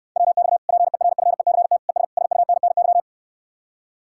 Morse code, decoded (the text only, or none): WX PERFECT ES SUNNY